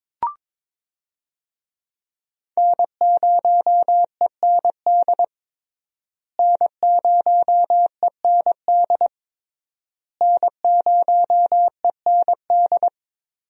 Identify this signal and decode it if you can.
Morse code, heard as a keyed tone, about 22 wpm, 700 Hz. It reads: N0END N0END N0END
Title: USA Call Signs - 5 Characters - Repeated 3x - 22wpm